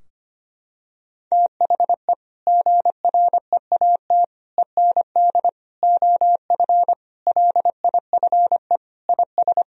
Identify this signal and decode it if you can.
Morse code, heard as a keyed tone, about 25 wpm, 700 Hz. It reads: THE GREAT END OF LIFE IS
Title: Sentences from Top 1000 Words - Review Part 4 - 25wpm